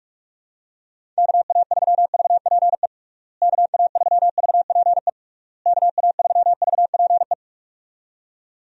Morse code, heard as a keyed tone, about 41 wpm, 700 Hz. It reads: KA3VPE KA3VPE KA3VPE
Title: USA Call Signs - 6 Characters - Repeated 3x - 45wpm